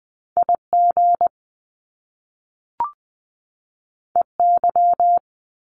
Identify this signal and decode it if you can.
Morse code, heard as a keyed tone, about 20 wpm, 700 Hz. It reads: I G E Y
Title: Top Two Letter Combinations in Words 20wpm